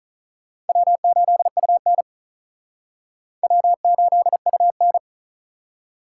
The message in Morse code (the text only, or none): W8UN W8UN